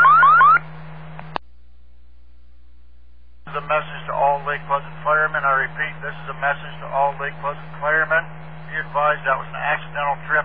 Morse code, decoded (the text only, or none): none